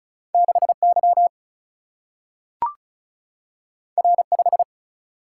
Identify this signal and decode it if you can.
Morse code, heard as a keyed tone, about 36 wpm, 700 Hz. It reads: 6Y R5
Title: Single-Letter Single-Number 35wpm